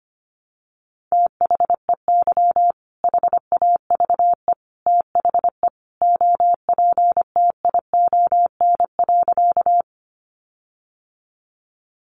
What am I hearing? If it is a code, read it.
Morse code, heard as a keyed tone, about 25 wpm, 700 Hz. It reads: THEY HAVE THE OPTION.
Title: Sentences from Top 1300 Words 25wpm